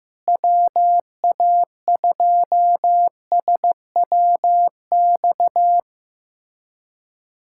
Morse code, heard as a keyed tone, about 15 wpm, 700 Hz. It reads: WA2SWX
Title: K1USN SST QSO Practice - 15wpm